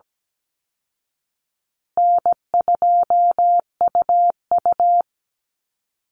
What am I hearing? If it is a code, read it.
Morse code, heard as a keyed tone, about 17 wpm, 700 Hz. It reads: N2UU
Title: Easy Call Signs Repeated 2x - 17wpm